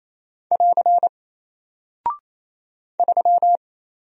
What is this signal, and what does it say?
Morse code, heard as a keyed tone, about 27 wpm, 700 Hz. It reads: <AR> 3
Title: Single Character (Letter:Number:Punctuation:Prosign) Alt 28wpm